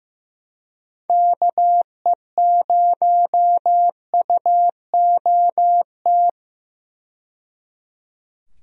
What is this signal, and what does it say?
Morse code, heard as a keyed tone, about 15 wpm, 700 Hz. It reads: KE0UOT
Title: USA Call Signs - 6 Characters - 15wpm